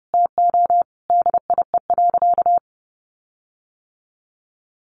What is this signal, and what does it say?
Morse code, heard as a keyed tone, about 29 wpm, 700 Hz. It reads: TO DIE.
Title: Sentences from Top 1300 Words 30wpm